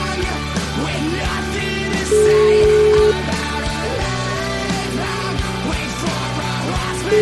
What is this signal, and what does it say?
Ringback tone (Europe, 425 Hz), with noise.